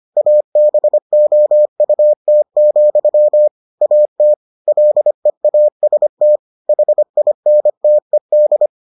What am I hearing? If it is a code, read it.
Morse code, heard as a keyed tone, about 25 wpm, 600 Hz. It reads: ABOUT, AT LEAST HINTED